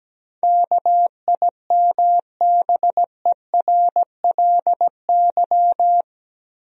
Morse code, heard as a keyed tone, about 17 wpm, 700 Hz. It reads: KIMBERLY
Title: USA Common Names - 17wpm